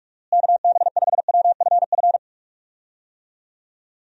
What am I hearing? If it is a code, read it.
Morse code, heard as a keyed tone, about 46 wpm, 700 Hz. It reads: KB5WFF
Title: USA Call Signs - 6 Characters - Repeated 3x - 45wpm